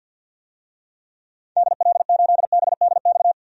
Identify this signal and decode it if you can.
Morse code, heard as a keyed tone, about 45 wpm, 700 Hz. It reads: DL8BDX
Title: DXCC-100 Call Signs - 6 Characters - Repeated 3x - 50wpm